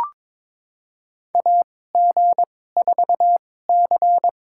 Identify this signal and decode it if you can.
Morse code, heard as a keyed tone, about 22 wpm, spaced as for 16 wpm, 700 Hz. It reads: AG4C
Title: USA Call Signs - 4 Characters - 2x Character Spacing - 22wpm